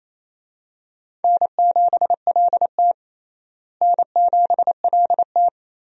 Morse code, heard as a keyed tone, about 28 wpm, 700 Hz. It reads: N7LT N7LT
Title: USA Call Signs - 4 Characters - Repeated 2x - 28wpm